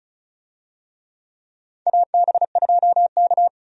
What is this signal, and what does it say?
Morse code, heard as a keyed tone, about 36 wpm, 700 Hz. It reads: AB2K
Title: USA Call Signs - 4 Characters - 35wpm